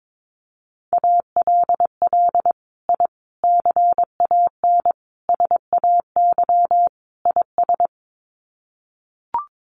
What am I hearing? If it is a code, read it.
Morse code, heard as a keyed tone, about 22 wpm, 700 Hz. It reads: ALL I CAN SAY IS
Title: Sets of 5 Words Encore2 22wpm